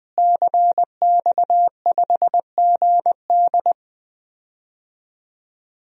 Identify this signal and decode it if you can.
Morse code, heard as a keyed tone, about 20 wpm, 700 Hz. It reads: CX5GD
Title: DXCC-100 Call Signs - 5 Characters - Repeated 2x - 20wpm